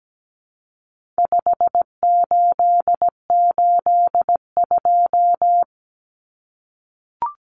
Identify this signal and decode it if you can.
Morse code, heard as a keyed tone, about 17 wpm, 700 Hz. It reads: 5882
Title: Numbers - 4-Digit 17wpm